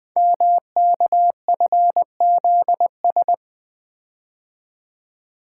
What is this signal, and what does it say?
Morse code, heard as a keyed tone, about 20 wpm, 700 Hz. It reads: MKFZS
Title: Code Groups - 5 Letters - 20wpm